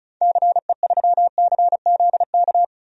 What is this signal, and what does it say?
Morse code, heard as a keyed tone, about 36 wpm, 700 Hz. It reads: CE3CZK
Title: DXCC-100 Call Signs - 6 Characters - 35wpm